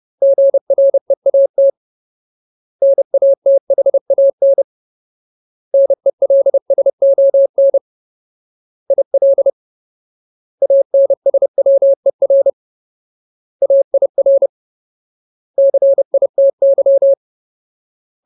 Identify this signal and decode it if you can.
Morse code, heard as a keyed tone, about 30 wpm, 550 Hz. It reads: GREAT NATHAN NELSON IL ANSWER AIR CITY
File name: Ultimate Word List / 30WPM Ultimate